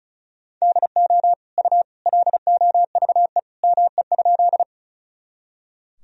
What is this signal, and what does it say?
Morse code, heard as a keyed tone, about 36 wpm, 700 Hz. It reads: DO U LOVE ME?
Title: Sentences from Top 400 Words and Abbreviations 35wpm